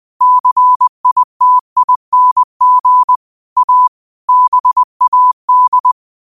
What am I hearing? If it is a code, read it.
Morse code, heard as a keyed tone, about 20 wpm, 1000 Hz. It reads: CITING A BAD
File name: 20wpm